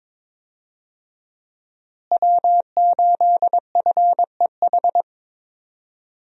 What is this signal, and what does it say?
Morse code, heard as a keyed tone, about 22 wpm, 700 Hz. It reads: W8FEH
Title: USA Call Signs - 5 Characters - Repeated 2x - 22wpm